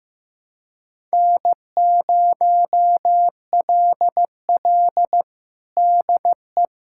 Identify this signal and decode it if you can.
Morse code, heard as a keyed tone, about 15 wpm, 700 Hz. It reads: N0LL DE
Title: USA Callsign DE Callsign 15wpm